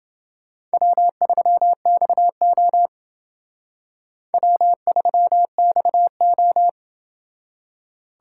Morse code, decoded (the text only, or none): W3XO W3XO